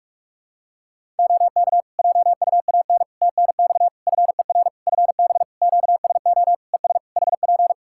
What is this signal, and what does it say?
Morse code, heard as a keyed tone, about 46 wpm, 700 Hz. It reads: OK JUAN TNX FER FB QSO ES HP
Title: Practice QSO - Exchange 4 - Ending - 45wpm